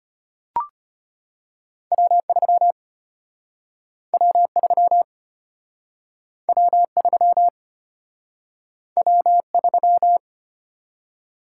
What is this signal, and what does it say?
Morse code, heard as a keyed tone, about 35 wpm, 700 Hz. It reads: W3 W3 W3 W3